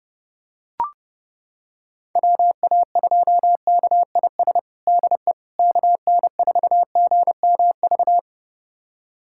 Morse code, heard as a keyed tone, about 30 wpm, 700 Hz. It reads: WA2KIS DE KN4GMV
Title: USA Callsign DE Callsign 30wpm